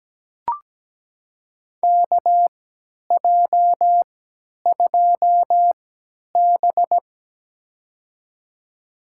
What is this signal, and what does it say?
Morse code, heard as a keyed tone, about 17 wpm, spaced as for 10 wpm, 700 Hz. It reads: KJ2B
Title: USA Call Signs - 3x Character Spacing - 17wpm